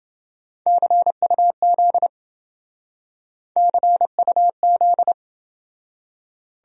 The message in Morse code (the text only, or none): CUZ CUZ